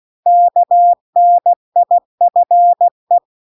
Morse code, heard as a keyed tone, about 16 wpm, 700 Hz. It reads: KNIFE